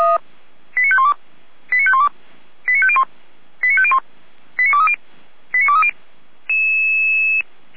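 Five-tone selective call (ZVEI), heard as quick runs of five tones, tones of about 70 ms. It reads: 89512 89512 89501 89501 89230 89230
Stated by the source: ZVEI five-tone selective call received on SDR, logged by Feuerwehr Orlamünde (f85360).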